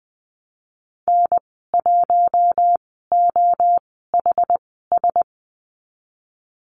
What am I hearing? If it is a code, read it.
Morse code, heard as a keyed tone, about 20 wpm, spaced as for 14 wpm, 700 Hz. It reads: N1OHS
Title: 076 Lesson 13g - Letter L - Call Signs 20wpm